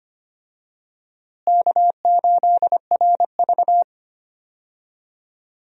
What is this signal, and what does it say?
Morse code, heard as a keyed tone, about 25 wpm, 700 Hz. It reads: K8RV
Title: USA Call Signs - 4 Characters - Repeated 2x - 25wpm